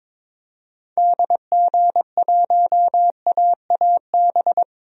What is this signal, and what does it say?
Morse code, heard as a keyed tone, about 22 wpm, 700 Hz. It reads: DG1AAB
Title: DXCC-100 Call Signs - 6 Characters - Repeated 2x - 22wpm